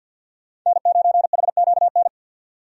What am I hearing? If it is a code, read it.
Morse code, heard as a keyed tone, about 50 wpm, 700 Hz. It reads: N9HQN
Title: USA Call Signs - 5 Characters - 50wpm